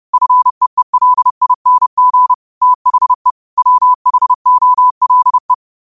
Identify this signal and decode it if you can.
Morse code, heard as a keyed tone, about 30 wpm, 1000 Hz. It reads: FEELING THE WHOLE